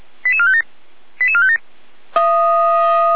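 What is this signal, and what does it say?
Five-tone selective call (ZVEI), heard as quick runs of five tones, tones of about 70 ms. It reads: 88457 88457
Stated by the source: ZVEI five-tone call received on an SDR, from audio.f85360.feuerwehr-orlamuende.org (Feuerwehr Orlamünde fire brigade).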